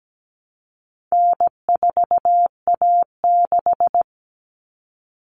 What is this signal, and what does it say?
Morse code, heard as a keyed tone, about 17 wpm, 700 Hz. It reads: N4A6
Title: Common Cut Numbers - 4-Digit 17wpm